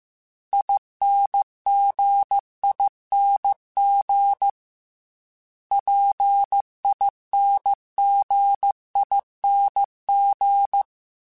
Morse code, heard as a keyed tone, about 15 wpm, 800 Hz. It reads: INGING PINGING